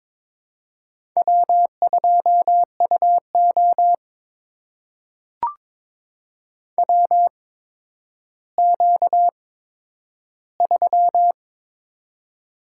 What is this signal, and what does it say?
Morse code, heard as a keyed tone, about 23 wpm, spaced as for 11 wpm, 700 Hz. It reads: W2UO WQ3
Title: USA Call Signs - 4 Characters - 8x Character Spacing - 22wpm